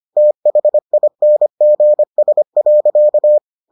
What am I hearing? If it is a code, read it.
Morse code, heard as a keyed tone, about 25 wpm, 600 Hz. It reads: THINGS.